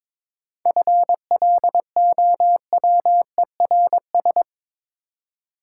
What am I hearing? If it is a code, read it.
Morse code, heard as a keyed tone, about 26 wpm, 700 Hz. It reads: FLOWERS
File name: Easy Words - Speed-Racing with Hints - 22wpm